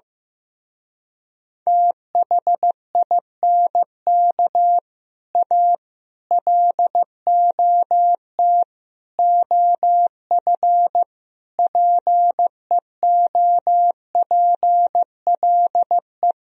Morse code, heard as a keyed tone, about 15 wpm, 700 Hz. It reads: THINK A LOT OF PEOPLE
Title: Speed-Racing - Sets of 5 Words 15wpm